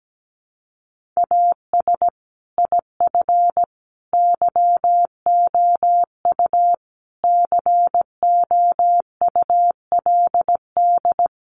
Morse code, heard as a keyed tone, about 17 wpm, 700 Hz. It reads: AS IF YOU COULD
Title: Sentences from Top 900 Words 17wpm